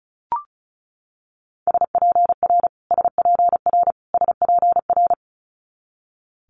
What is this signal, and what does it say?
Morse code, heard as a keyed tone, about 36 wpm, 700 Hz. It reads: SPR SPR SPR
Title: Multi-letter Phonemes Repeated 3x 35wpm